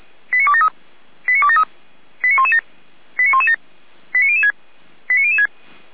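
Five-tone selective call (ZVEI), heard as quick runs of five tones, tones of about 70 ms. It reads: 89273 89273 89117 89117 89006 89006
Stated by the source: ZVEI five-tone selective call 89220 received with an SDR, served by feuerwehr-orlamuende.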